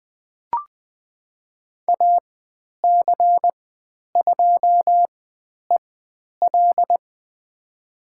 Morse code, heard as a keyed tone, about 20 wpm, spaced as for 10 wpm, 700 Hz. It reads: AC2EL
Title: 095 Lesson 16g - Letter c - Call Signs 20wpm